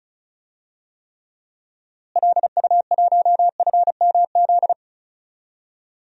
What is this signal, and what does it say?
Morse code, heard as a keyed tone, about 34 wpm, 700 Hz. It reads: LU1FMZ
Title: DXCC-100 Call Signs - 6 Characters - Repeated 3x - 35wpm